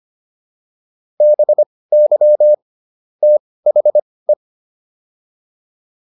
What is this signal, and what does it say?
Morse code, heard as a keyed tone, about 25 wpm, spaced as for 18 wpm, 600 Hz. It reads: BY THE